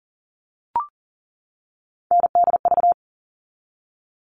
Morse code, heard as a keyed tone, about 40 wpm, 700 Hz. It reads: NDV
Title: All Cut Numbers - 3-Digit 40wpm